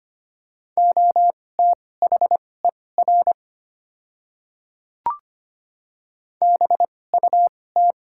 Morse code, heard as a keyed tone, about 25 wpm, spaced as for 18 wpm, 700 Hz. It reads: OTHER BUT
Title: 2x Character Spacing - Top 100 Words - 25wpm